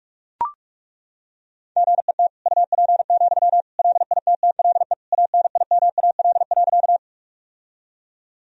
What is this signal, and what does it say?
Morse code, heard as a keyed tone, about 41 wpm, 700 Hz. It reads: GET UP, LITTLE ANIMAL.
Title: Sentences from Top 1000 Words - Review Part 1 - 45wpm